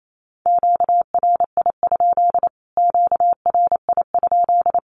Morse code, heard as a keyed tone, about 28 wpm, 700 Hz. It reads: QRI? QRI?
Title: Q-Codes Repeated 3x 28wpm